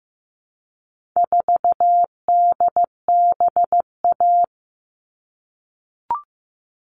Morse code, heard as a keyed tone, about 15 wpm, 700 Hz. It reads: 4DBA
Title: All Cut Numbers - 4-Digit 15wpm